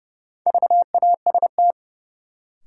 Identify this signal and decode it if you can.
Morse code, heard as a keyed tone, about 32 wpm, 700 Hz. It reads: VAST
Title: Sentences from Top 1300 Words - New Word Drill - 30wpm